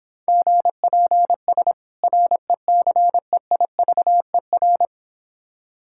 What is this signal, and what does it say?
Morse code, heard as a keyed tone, about 26 wpm, 700 Hz. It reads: GPS RECEIVER